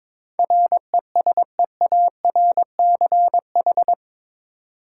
Morse code, heard as a keyed tone, about 22 wpm, 700 Hz. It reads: RESEARCH
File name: practice / 22wpm / Eight Letter Words - 22wpm